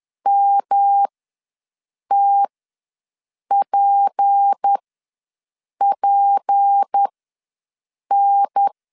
Morse code, heard as a keyed tone, about 10 wpm, spaced as for 6 wpm, 800 Hz. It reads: MTPPN
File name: LEZIONI / 6 PKX0 / 6 P ws 10wpm 3751